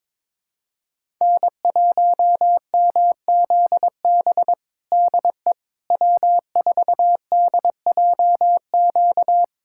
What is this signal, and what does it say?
Morse code, heard as a keyed tone, about 22 wpm, 700 Hz. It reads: N1MZB DE W4DJQ
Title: USA Callsign DE Callsign - 5 Characters - 22wpm